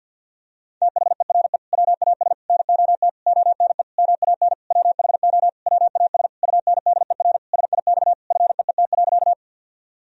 Morse code, heard as a keyed tone, about 47 wpm, 700 Hz. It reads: THERE WAS NOT ONE MAN WHO WAS UNDER SIX FEET.
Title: Sentences from Top 1000 Words - Review Part 2 - 50wpm